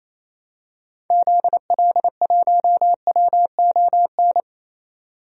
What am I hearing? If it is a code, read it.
Morse code, heard as a keyed tone, about 27 wpm, 700 Hz. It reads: ZL1WON
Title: DXCC-100 Call Signs - 6 Characters - 28wpm